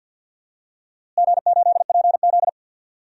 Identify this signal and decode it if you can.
Morse code, heard as a keyed tone, about 48 wpm, 700 Hz. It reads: G8PZ
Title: DXCC-100 Call Signs - 4 Characters - Repeated 2x - 50wpm